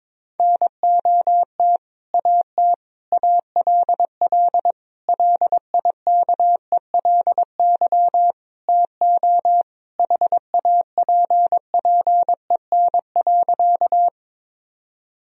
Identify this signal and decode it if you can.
Morse code, heard as a keyed tone, about 22 wpm, 700 Hz. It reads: NOT AT ALL LIKELY TO HAPPEN.
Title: Sentences from Top 1000 Words - Review Part 4 - 22wpm